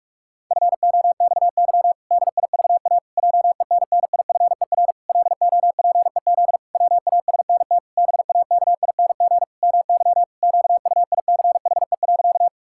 Morse code, heard as a keyed tone, about 46 wpm, 700 Hz. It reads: FOXY DIVA JENNIFER LOPEZ WASNT BAKING MY QUICHE.
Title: Pangram Challenge v2 - 45wpm